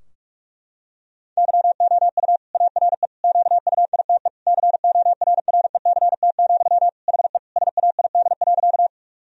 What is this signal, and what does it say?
Morse code, heard as a keyed tone, about 41 wpm, 700 Hz. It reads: YOU ARE QUITE CORRECT, HE SAID.
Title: Sentences from Top 1000 Words - Review Part 2 - 45wpm